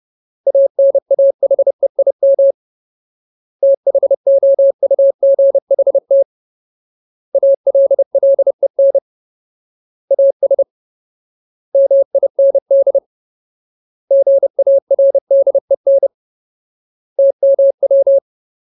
Morse code, heard as a keyed tone, about 30 wpm, 550 Hz. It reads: ANAHEIM THOUGHT ALLEN AS MIND GARDEN TMW
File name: Ultimate Word List / 30WPM Ultimate